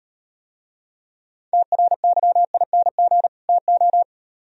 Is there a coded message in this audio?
Yes: Morse code from 1.5 to 4.0 s, heard as a keyed tone, about 38 wpm, 700 Hz. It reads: TRYING TO